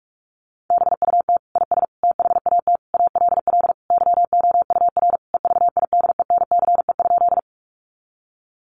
Morse code, heard as a keyed tone, about 44 wpm, 700 Hz. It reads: BUT IS THAT ALL YOUR EVIDENCE?
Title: Sentences from Top 1300 Words 45wpm